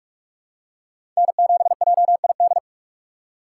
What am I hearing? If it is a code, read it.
Morse code, heard as a keyed tone, about 46 wpm, 700 Hz. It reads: N7JID
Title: USA Call Signs - 4x Character Spacing - 45wpm